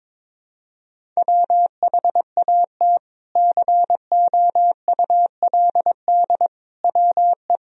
Morse code, heard as a keyed tone, about 22 wpm, 700 Hz. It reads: WHAT COULD WE
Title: Sentences from Top 400 Words and Abbreviations 22wpm